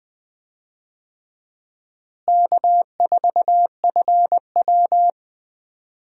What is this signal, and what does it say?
Morse code, heard as a keyed tone, about 20 wpm, 700 Hz. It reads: K4FW
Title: USA Call Signs - 4 Characters - Repeated 2x - 20wpm